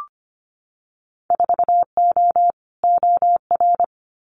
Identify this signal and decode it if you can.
Morse code, heard as a keyed tone, about 25 wpm, 700 Hz. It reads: 4O OR
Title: Winter Field Day Exchanges Encore 25wpm